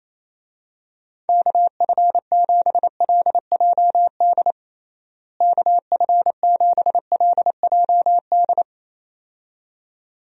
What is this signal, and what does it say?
Morse code, heard as a keyed tone, about 28 wpm, 700 Hz. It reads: KF7LJD KF7LJD
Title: USA Call Signs - 6 Characters - Repeated 2x - 28wpm